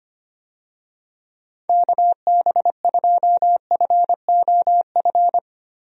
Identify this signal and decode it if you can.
Morse code, heard as a keyed tone, about 25 wpm, 700 Hz. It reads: KB2FOF
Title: USA Call Signs - 6 Characters - Repeated 2x - 25wpm